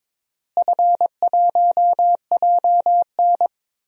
Morse code, heard as a keyed tone, about 22 wpm, 700 Hz. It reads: F1JN
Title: DXCC-100 Call Signs - 4 Characters - Repeated 3x - 22wpm